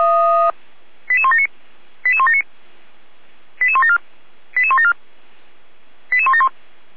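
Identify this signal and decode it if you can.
Five-tone selective call (ZVEI), heard as quick runs of five tones, tones of about 70 ms. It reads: 88179 88179 88174 88174 88171
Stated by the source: ZVEI five-tone selective call received on an SDR.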